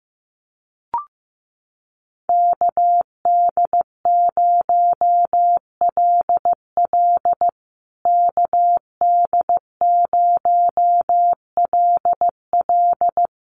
Morse code, heard as a keyed tone, about 14 wpm, 700 Hz. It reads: KD0LL KD0LL
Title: USA Call Signs Repeated 3x 15wpm